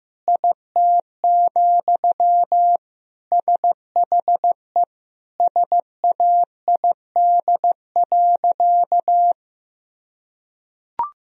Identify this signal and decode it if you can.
Morse code, heard as a keyed tone, about 15 wpm, 700 Hz. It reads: IT, SHE SAID.
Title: Sentences from Top 1000 Words - Review Part 4 - 15wpm